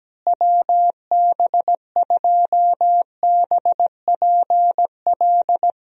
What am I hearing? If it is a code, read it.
Morse code, heard as a keyed tone, about 17 wpm, 700 Hz. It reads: WB2BPL
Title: USA Call Signs - 6 Characters - Repeated 3x - 17wpm